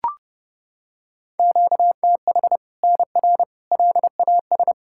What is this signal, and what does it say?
Morse code, heard as a keyed tone, about 32 wpm, 700 Hz. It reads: QTH NR LAS
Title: Practice QSO - Exchange 1 - Beginning - 30wpm